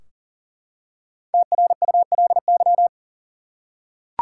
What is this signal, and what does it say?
Morse code, heard as a keyed tone, about 40 wpm, 700 Hz. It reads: TRULY
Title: Five Letter Words - 40wpm